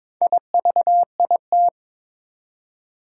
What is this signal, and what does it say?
Morse code, heard as a keyed tone, about 22 wpm, 700 Hz. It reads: IVIT
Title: Four Challenging Letter-Number Combinations 22wpm